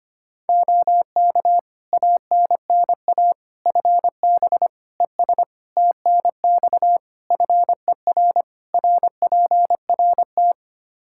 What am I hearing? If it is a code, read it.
Morse code, heard as a keyed tone, about 25 wpm, 700 Hz. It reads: OK ANNA FB ES TNX FER RPRT
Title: Practice QSO - Exchange 2 - Info - 25wpm